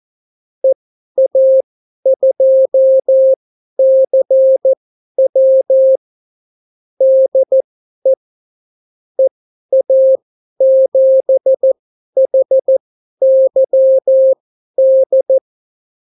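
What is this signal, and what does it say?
Morse code, heard as a keyed tone, about 14 wpm, spaced as for 11 wpm, 530 Hz. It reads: EA2CW DE EA7HYD
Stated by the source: Example Spanish CW QSO from a Morse learning book, 14 wpm characters with 8 wpm spacing.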